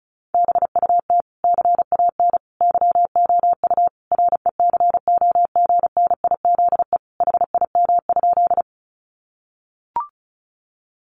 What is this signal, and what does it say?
Morse code, heard as a keyed tone, about 34 wpm, 700 Hz. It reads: BUT CAN YOU RECOGNIZE HIM?
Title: Sentences from Top 1400 Words 35wpm